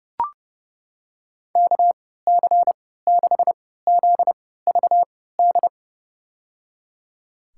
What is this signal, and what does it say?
Morse code, heard as a keyed tone, about 32 wpm, spaced as for 17 wpm, 700 Hz. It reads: KC6ZVD
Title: USA Call Signs - 6 Characters - 3x Character Spacing - 30wpm